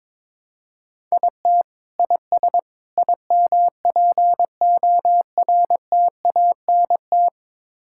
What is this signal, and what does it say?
Morse code, heard as a keyed tone, about 22 wpm, 700 Hz. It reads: IT IS IMPORTANT
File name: practice / 22wpm / Build A Phrase - 5-Word Phrases - 22wpm